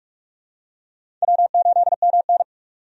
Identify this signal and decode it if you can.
Morse code, heard as a keyed tone, about 46 wpm, 700 Hz. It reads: W8MN
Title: USA Call Signs - 4 Characters - 45wpm